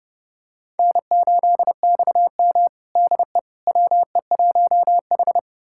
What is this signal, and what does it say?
Morse code, heard as a keyed tone, about 30 wpm, 700 Hz. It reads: N8XM DE WE1H
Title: USA Callsign DE Callsign - 4 Characters - 30wpm